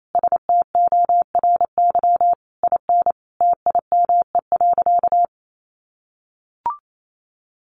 Morse code, heard as a keyed tone, about 28 wpm, 700 Hz. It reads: STORY IN TIME.
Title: Sentences from Top 1400 Words 28wpm